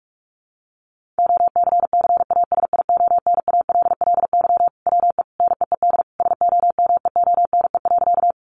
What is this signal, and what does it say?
Morse code, heard as a keyed tone, about 44 wpm, 700 Hz. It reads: OCCASIONALLY WE NEED SOMEONE.
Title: Sentences from Top 1300 Words 45wpm